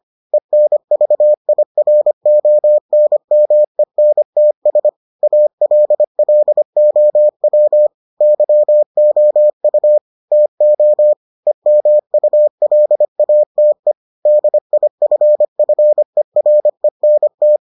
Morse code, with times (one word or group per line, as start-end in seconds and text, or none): 0.3-4.9 ENVIRONMENTS
5.2-7.9 ALLOW
8.2-10.0 YOU
10.3-11.1 TO
11.5-13.9 EMULATE
14.3-17.6 DIFFERENT